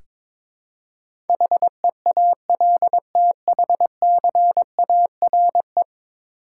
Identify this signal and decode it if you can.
Morse code, heard as a keyed tone, about 22 wpm, 700 Hz. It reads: HEALTHCARE
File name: practice / 22wpm / 10 Letter Words - First 3 Letters Given - 22wpm